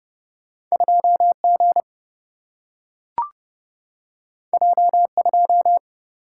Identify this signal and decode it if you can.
Morse code, heard as a keyed tone, about 30 wpm, 700 Hz. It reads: 2 G J 2
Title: Single-Letter Single-Number 30wpm